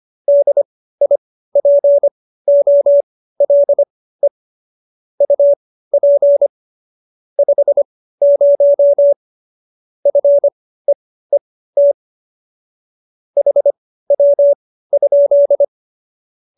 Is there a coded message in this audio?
Yes: Morse code from 0.3 to 15.6 s, heard as a keyed tone, about 25 wpm, spaced as for 15 wpm, 570 Hz. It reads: DIPOLE UP 50 FEET HW?